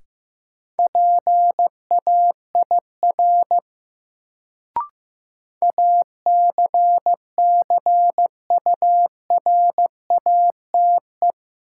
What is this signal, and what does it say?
Morse code, heard as a keyed tone, about 15 wpm, 700 Hz. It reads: PAIR ACCURATE
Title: Sentences from Top 1200 Words - New Word Drill - 15wpm